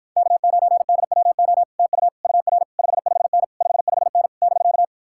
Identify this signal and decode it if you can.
Morse code, heard as a keyed tone, about 59 wpm, 700 Hz. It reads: K9DWO TU UR 55N 55N <BK>